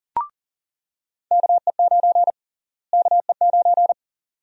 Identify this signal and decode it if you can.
Morse code, heard as a keyed tone, about 40 wpm, 700 Hz. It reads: KE9 KE9
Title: USA Call Sign Prefixes - Repeated 2x - 40wpm